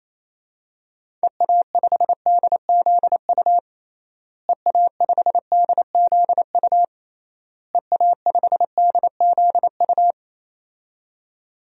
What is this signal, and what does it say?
Morse code, heard as a keyed tone, about 28 wpm, 700 Hz. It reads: EA5DZU EA5DZU EA5DZU
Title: DXCC-100 Call Signs - 6 Characters - Repeated 3x - 28wpm